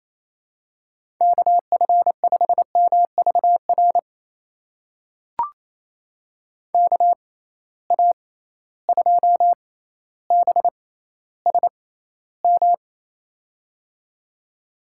Morse code, heard as a keyed tone, about 27 wpm, spaced as for 14 wpm, 700 Hz. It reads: KF5MVR KA2BSM
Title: USA Call Signs - 6 Characters - 6x Character Spacing - 28wpm